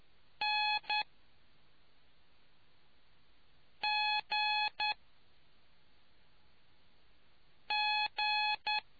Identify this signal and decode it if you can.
Morse code, heard as a keyed tone, about 10 wpm, spaced as for 3 wpm, 850 Hz. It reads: NGG